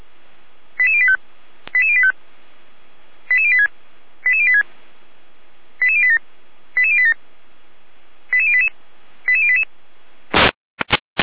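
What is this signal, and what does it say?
Five-tone selective call (ZVEI), heard as quick runs of five tones, tones of about 70 ms. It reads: 88085 88085 88086 88086 88087 88087 88088 88088
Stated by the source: ZVEI five-tone call 88083 received on SDR.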